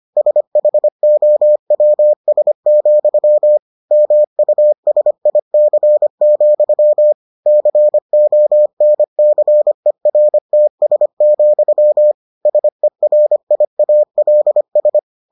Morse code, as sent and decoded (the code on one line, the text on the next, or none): ... .... --- .-- ... --..-- / -- ..- ... .. -.-. --..-- / -.-. --- -. -.-. .-. - ... --..-- / ... .-. .. .- .-.. ...
SHOWS, MUSIC, CONCERTS, SERIALS